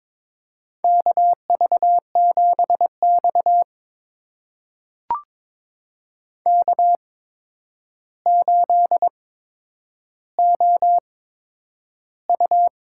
Morse code, heard as a keyed tone, about 22 wpm, spaced as for 9 wpm, 700 Hz. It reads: KV7X K8OU
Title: USA Call Signs - 8x Character Spacing - 22wpm